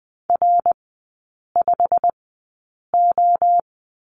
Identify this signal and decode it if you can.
Morse code, heard as a keyed tone, about 20 wpm, spaced as for 8 wpm, 700 Hz. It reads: R5O
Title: 087 Lesson 15e - Number 5 - Three Characters 20wpm